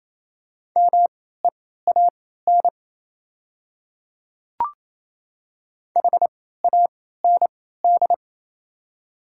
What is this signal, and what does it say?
Morse code, heard as a keyed tone, about 28 wpm, spaced as for 16 wpm, 700 Hz. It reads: MEAN HAND
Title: 3x Character Spacing - Top 200 Words - 28wpm